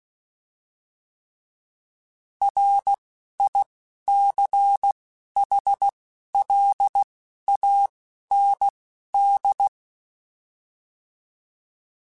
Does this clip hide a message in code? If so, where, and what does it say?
Yes: Morse code from 2.4 to 9.7 s, heard as a keyed tone, about 16 wpm, spaced as for 11 wpm, 800 Hz. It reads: RICHLAND